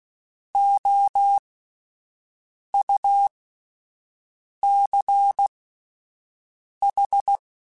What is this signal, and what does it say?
Morse code, heard as a keyed tone, about 16 wpm, spaced as for 5 wpm, 800 Hz. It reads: OUCH
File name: Alphabetical / Alphabetical mp3s / Os @5